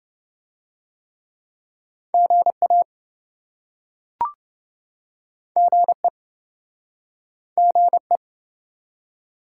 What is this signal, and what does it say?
Morse code, heard as a keyed tone, about 29 wpm, 700 Hz. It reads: GA GE GE